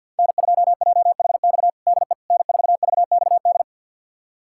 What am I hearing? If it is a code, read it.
Morse code, heard as a keyed tone, about 53 wpm, 700 Hz. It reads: N2JHX DE N4VXD
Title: USA Callsign DE Callsign - 5 Characters - 50wpm